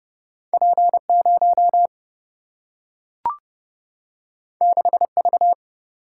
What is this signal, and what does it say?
Morse code, heard as a keyed tone, about 30 wpm, 700 Hz. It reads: P0 6V
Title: Single-Letter Single-Number 30wpm